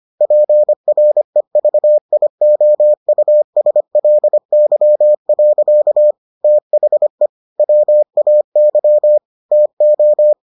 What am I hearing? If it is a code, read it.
Morse code, heard as a keyed tone, about 25 wpm, 600 Hz. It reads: PREVIOUSLY. THE WAY TO